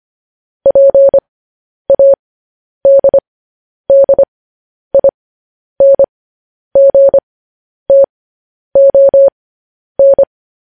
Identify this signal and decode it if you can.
Morse code, heard as a keyed tone, about 25 wpm, spaced as for 10 wpm, 550 Hz. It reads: PADDINGTON
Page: news headlines in morse code at 10 wpm